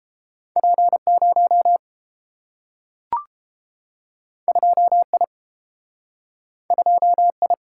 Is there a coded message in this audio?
Yes: Morse code from 0.6 to 7.5 s, heard as a keyed tone, about 30 wpm, 700 Hz. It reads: P0 2I 2I